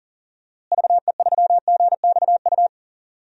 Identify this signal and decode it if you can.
Morse code, heard as a keyed tone, about 40 wpm, 700 Hz. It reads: VE3GXU
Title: DXCC-100 Call Signs - 6 Characters - Repeated 2x - 40wpm